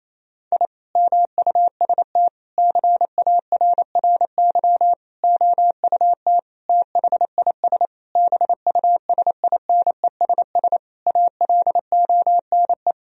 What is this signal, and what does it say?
Morse code, heard as a keyed tone, about 28 wpm, 700 Hz. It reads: I MUST CARRY OUT THIS BUSINESS ALONE
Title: Sentences from Top 1000 Words - Review Part 2 - 28wpm